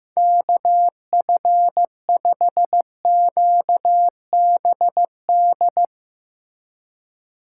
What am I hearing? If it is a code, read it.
Morse code, heard as a keyed tone, about 15 wpm, 700 Hz. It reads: KF5QBD
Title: USA Call Signs - 6 Characters - 3x Character Spacing - 15wpm